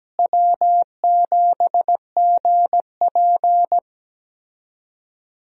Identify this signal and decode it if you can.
Morse code, heard as a keyed tone, about 17 wpm, 700 Hz. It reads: W7GP
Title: USA Call Signs - 4 Characters - Repeated 3x - 17wpm